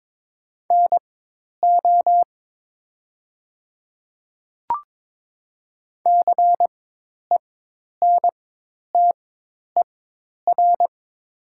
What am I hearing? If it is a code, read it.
Morse code, heard as a keyed tone, about 22 wpm, spaced as for 10 wpm, 700 Hz. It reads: NO CENTER
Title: 4x Character Spacing - Top 500 Words - 22wpm